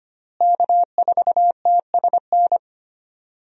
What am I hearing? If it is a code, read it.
Morse code, heard as a keyed tone, about 25 wpm, 700 Hz. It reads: K4TSN
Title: USA Call Signs - 5 Characters - Repeated 2x - 25wpm